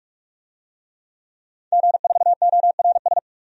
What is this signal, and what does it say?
Morse code, heard as a keyed tone, about 42 wpm, 700 Hz. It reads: G4ORS
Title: DXCC-100 Call Signs - 5 Characters - 45wpm